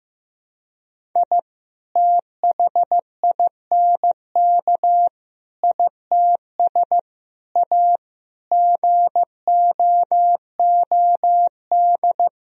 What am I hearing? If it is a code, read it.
Morse code, heard as a keyed tone, about 16 wpm, 700 Hz. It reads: I THINK ITS A GOOD
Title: Sentences from Top 1000 Words - Review Part 1 - 15wpm